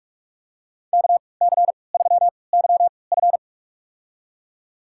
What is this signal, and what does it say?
Morse code, heard as a keyed tone, about 44 wpm, spaced as for 25 wpm, 700 Hz. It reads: KC3YF
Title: USA Call Signs - 5 Characters - 3x Character Spacing - 45wpm